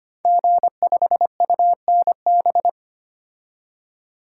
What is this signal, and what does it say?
Morse code, heard as a keyed tone, about 25 wpm, 700 Hz. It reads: G5UNB
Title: Code Groups - 5 Letter:Numbers - 25wpm